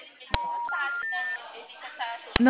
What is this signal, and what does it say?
Special information tone, with speech.